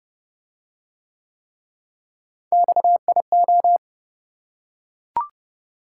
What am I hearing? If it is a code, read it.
Morse code, heard as a keyed tone, about 27 wpm, 700 Hz. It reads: XIO